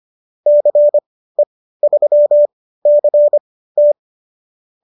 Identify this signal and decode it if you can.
Morse code, heard as a keyed tone, about 25 wpm, spaced as for 15 wpm, 590 Hz. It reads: CE3CT